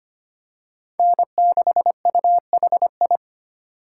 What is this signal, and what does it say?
Morse code, heard as a keyed tone, about 25 wpm, 700 Hz. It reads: N6UHI
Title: USA Call Signs - 5 Characters - 4x Character Spacing - 25wpm